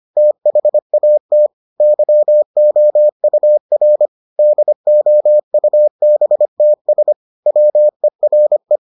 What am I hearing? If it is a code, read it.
Morse code, heard as a keyed tone, about 25 wpm, 600 Hz. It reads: THAT YOUR DOUBTS WERE